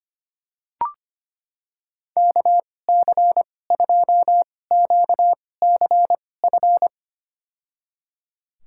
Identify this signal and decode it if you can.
Morse code, heard as a keyed tone, about 25 wpm, spaced as for 18 wpm, 700 Hz. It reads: KC2QCF
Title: USA Call Signs - 6 Characters - 2x Character Spacing - 25wpm